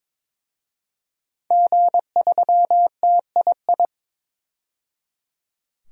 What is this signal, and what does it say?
Morse code, heard as a keyed tone, about 22 wpm, 700 Hz. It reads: G3TII
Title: DXCC-100 Call Signs - 5 Characters - 22wpm